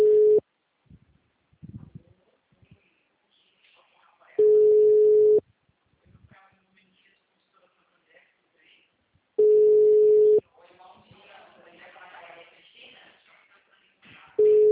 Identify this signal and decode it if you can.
Ringback tone (Europe, 425 Hz), with speech.